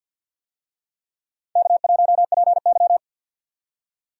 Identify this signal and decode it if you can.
Morse code, heard as a keyed tone, about 53 wpm, 700 Hz. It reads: K1PY
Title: USA Call Signs - 4 Characters - 50wpm